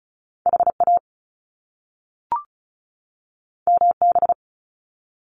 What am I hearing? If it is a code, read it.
Morse code, heard as a keyed tone, about 36 wpm, 700 Hz. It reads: HA MB
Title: Top Two Letter Combinations in Words 35wpm